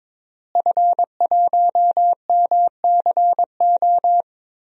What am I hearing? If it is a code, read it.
Morse code, heard as a keyed tone, about 22 wpm, 700 Hz. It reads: F1MCO
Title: DXCC-100 Call Signs - 5 Characters - Repeated 2x - 22wpm